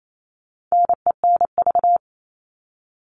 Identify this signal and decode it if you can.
Morse code, heard as a keyed tone, about 28 wpm, 700 Hz. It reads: NENV
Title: All Cut Numbers - 4-Digit 28wpm